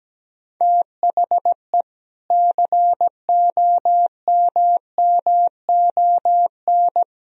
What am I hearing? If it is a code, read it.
Morse code, heard as a keyed tone, about 17 wpm, 700 Hz. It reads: THE COMMON
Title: Sentences from Top 1000 Words - Review Part 3 - 17wpm